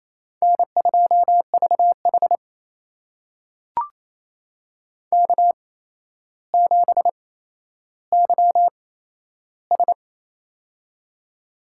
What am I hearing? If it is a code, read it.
Morse code, heard as a keyed tone, about 27 wpm, spaced as for 12 wpm, 700 Hz. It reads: N2VH K7YS
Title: USA Call Signs - 4 Characters - 8x Character Spacing - 28wpm